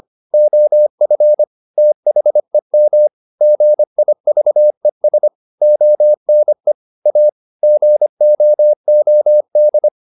Morse code, heard as a keyed tone, about 25 wpm, 600 Hz. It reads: OF THEM GIVES ONE A GOOD